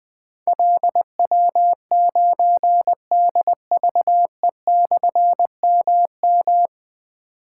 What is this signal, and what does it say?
Morse code, heard as a keyed tone, about 20 wpm, 700 Hz. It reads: LW9DVE/MM
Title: Difficult Call Signs 20wpm